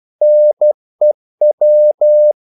Morse code, heard as a keyed tone, about 12 wpm, 600 Hz. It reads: NEW
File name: CW / Practice / NationalCapitals12